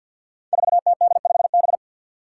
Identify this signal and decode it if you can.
Morse code, heard as a keyed tone, about 52 wpm, 700 Hz. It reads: 4TD5B
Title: Five Challenging Letter-Number Combinations 50wpm